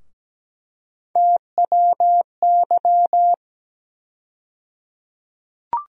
Morse code, heard as a keyed tone, about 17 wpm, 700 Hz. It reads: TWY
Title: USA Call Sign Postfixes - 3 Letters 17wpm